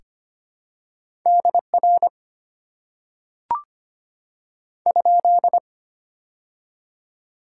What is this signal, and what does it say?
Morse code, heard as a keyed tone, about 25 wpm, 690 Hz. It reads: DR ?
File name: Top 90 QSO Elements - 6x Character Spacing - 25wpm